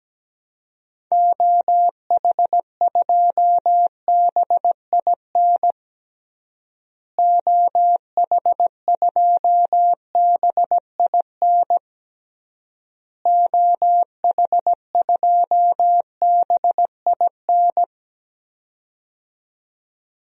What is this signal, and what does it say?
Morse code, heard as a keyed tone, about 17 wpm, 700 Hz. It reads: OH2BIN OH2BIN OH2BIN
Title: DXCC-100 Call Signs - 6 Characters - Repeated 3x - 17wpm